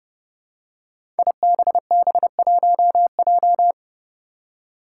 Morse code, heard as a keyed tone, about 29 wpm, 700 Hz. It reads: IBB1J
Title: Five Challenging Letter-Number Combinations 30wpm